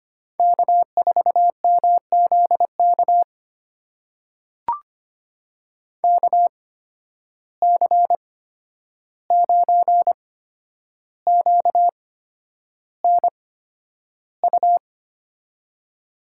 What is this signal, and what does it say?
Morse code, heard as a keyed tone, about 25 wpm, spaced as for 10 wpm, 700 Hz. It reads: K4MZK KC9QNU
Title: USA Call Signs - 8x Character Spacing - 25wpm